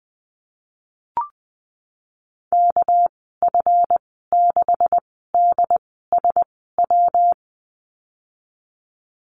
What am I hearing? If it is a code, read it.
Morse code, heard as a keyed tone, about 20 wpm, spaced as for 14 wpm, 700 Hz. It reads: KF6DSW
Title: 245 Lesson 39g - Letter z - Call Signs 20wpm